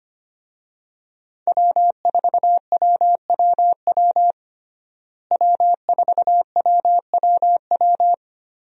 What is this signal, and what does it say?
Morse code, heard as a keyed tone, about 25 wpm, 700 Hz. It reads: W4WWW W4WWW
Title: USA Call Signs - 5 Characters - Repeated 2x - 25wpm